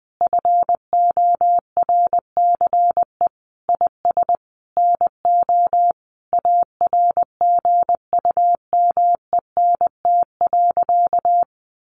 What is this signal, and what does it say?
Morse code, heard as a keyed tone, about 20 wpm, 700 Hz. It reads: FORCE IS NO ARGUMENT.
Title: Sentences from Top 1400 Words 20wpm